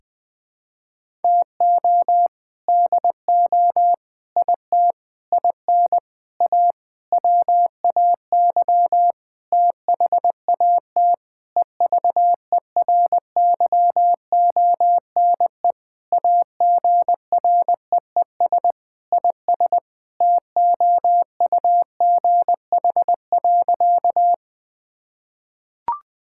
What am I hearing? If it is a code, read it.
Morse code, heard as a keyed tone, about 20 wpm, 700 Hz. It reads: TO DO IT IN A WAY THAT EVERYONE AGREES IS TOUGH.
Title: Sentences with 7-Word Phrases - 20wpm